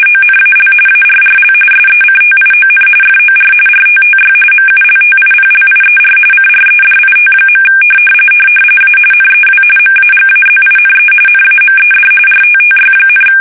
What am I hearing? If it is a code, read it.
RTTY, about 45.45 baud. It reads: ONAL ARRL SECTION FOR RTTY MULTIPLIER CREDIT. A NEW MAGAZINE NAMED RTTY,
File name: fsk850 / teletype-RTTY